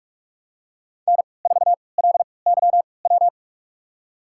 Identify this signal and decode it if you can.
Morse code, heard as a keyed tone, about 46 wpm, spaced as for 26 wpm, 700 Hz. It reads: N4LYW
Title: USA Call Signs - 5 Characters - 3x Character Spacing - 45wpm